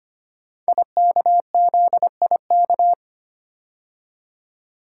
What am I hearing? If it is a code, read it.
Morse code, heard as a keyed tone, about 25 wpm, 700 Hz. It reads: IKZIK
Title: Code Groups - 5 Letters - 25wpm